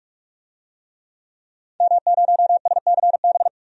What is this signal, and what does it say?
Morse code, heard as a keyed tone, about 44 wpm, 700 Hz. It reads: M0SCB
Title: DXCC-100 Call Signs - 5 Characters - 45wpm